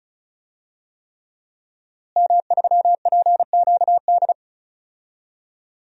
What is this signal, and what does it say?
Morse code, heard as a keyed tone, about 34 wpm, 700 Hz. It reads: M3PQD
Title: DXCC-100 Call Signs - 5 Characters - Repeated 3x - 35wpm